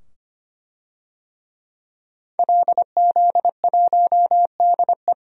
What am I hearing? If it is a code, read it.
Morse code, heard as a keyed tone, about 25 wpm, 700 Hz. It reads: LZ1DE
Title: DXCC-100 Call Signs - 5 Characters - 25wpm